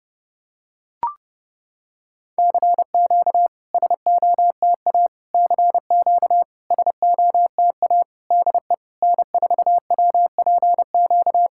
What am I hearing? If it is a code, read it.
Morse code, heard as a keyed tone, about 30 wpm, 700 Hz. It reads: CQ SOTA CQ SOTA DE N4WPQ
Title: SOTA QSO Practice 30wpm